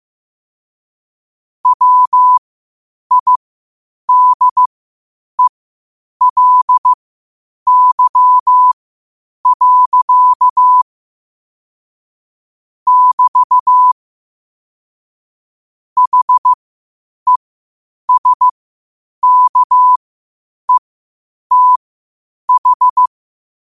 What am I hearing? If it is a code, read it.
Morse code, heard as a keyed tone, about 15 wpm, spaced as for 8 wpm, 1000 Hz. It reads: WIDELY. <BT> HESKETH
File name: qotdfiles / 8wpm